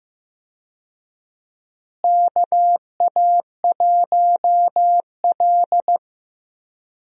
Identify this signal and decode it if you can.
Morse code, heard as a keyed tone, about 15 wpm, 700 Hz. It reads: KA1L
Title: USA Call Signs - 4 Characters - Repeated 2x - 15wpm